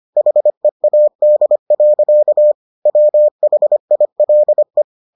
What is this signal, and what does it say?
Morse code, heard as a keyed tone, about 25 wpm, 600 Hz. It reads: HEAD. WHILE